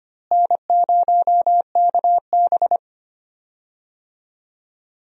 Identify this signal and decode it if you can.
Morse code, heard as a keyed tone, about 25 wpm, 700 Hz. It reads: N0KB
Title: USA Call Signs - 4 Characters - Repeated 3x - 25wpm